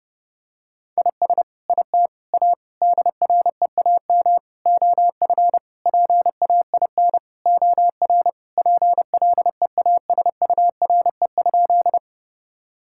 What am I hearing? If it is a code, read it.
Morse code, heard as a keyed tone, about 30 wpm, 700 Hz. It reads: IS IT A DREAM OF PAIN OR PLEASURE?
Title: Sentences from Top 1600 Words 30wpm